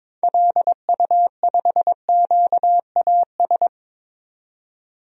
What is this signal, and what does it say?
Morse code, heard as a keyed tone, about 22 wpm, 700 Hz. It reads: LU5QAS
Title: DXCC-100 Call Signs - 6 Characters - 22wpm